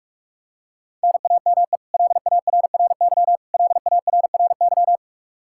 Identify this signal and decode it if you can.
Morse code, heard as a keyed tone, about 46 wpm, 700 Hz. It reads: NAME LARRY LARRY
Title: Practice QSO - Exchange 1 - Beginning - 45wpm